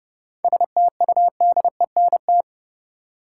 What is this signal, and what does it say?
Morse code, heard as a keyed tone, about 30 wpm, 700 Hz. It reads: STUDENT
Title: Sentences from Top 1000 Words - New Word Drill - 30wpm